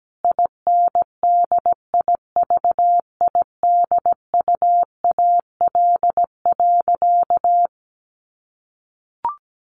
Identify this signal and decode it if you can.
Morse code, heard as a keyed tone, about 17 wpm, 700 Hz. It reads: INDIVIDUAL.
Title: Sentences from Top 1400 Words 17wpm